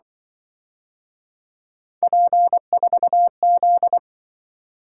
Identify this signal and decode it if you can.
Morse code, heard as a keyed tone, about 24 wpm, 700 Hz. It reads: P4Z
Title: Speed-Racing - Three Letter-Number Combinations - 20wpm